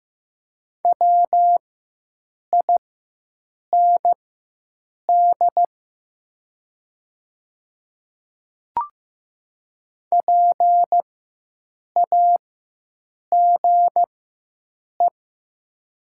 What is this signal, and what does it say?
Morse code, heard as a keyed tone, about 15 wpm, spaced as for 7 wpm, 700 Hz. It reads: WIND PAGE
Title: 4x Character Spacing - Top 500 Words - 15wpm